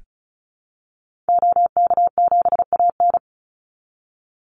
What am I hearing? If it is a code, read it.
Morse code, heard as a keyed tone, about 34 wpm, 700 Hz. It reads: OK7AN
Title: Call Signs Repeated 2x - 35wpm